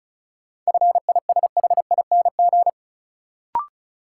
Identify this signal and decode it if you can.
Morse code, heard as a keyed tone, about 36 wpm, 700 Hz. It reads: FISHING
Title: Sentences from Top 1600 Words - New Word Drill Alt - 35wpm